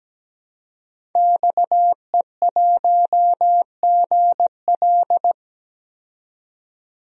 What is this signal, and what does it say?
Morse code, heard as a keyed tone, about 17 wpm, 700 Hz. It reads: XE1GL
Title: DXCC-100 Call Signs - 5 Characters - 17wpm